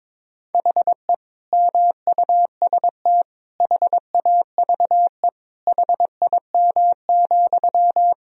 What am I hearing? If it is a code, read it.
Morse code, heard as a keyed tone, about 22 wpm, 700 Hz. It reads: HE MUST HAVE HIM,
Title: Sentences from Top 1000 Words - Review Part 4 - 22wpm